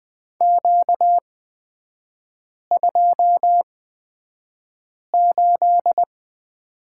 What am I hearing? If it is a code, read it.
Morse code, heard as a keyed tone, about 20 wpm, spaced as for 5 wpm, 690 Hz. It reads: Q28
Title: 236 Lesson 38e - Letter q - Three Characters 20wpm